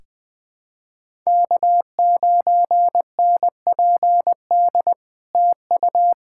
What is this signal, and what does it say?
Morse code, heard as a keyed tone, about 20 wpm, 700 Hz. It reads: K9NPD TU
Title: SOTA QSO Practice 20wpm